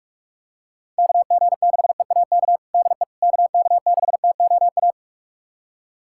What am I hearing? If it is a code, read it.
Morse code, heard as a keyed tone, about 46 wpm, 700 Hz. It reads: KG6EAK DE KK6TOA
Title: USA Callsign DE Callsign - 6 Characters - 45wpm